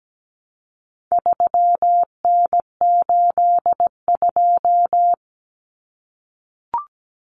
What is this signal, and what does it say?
Morse code, heard as a keyed tone, about 17 wpm, 700 Hz. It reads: 3N82
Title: Common Cut Numbers - 4-Digit 17wpm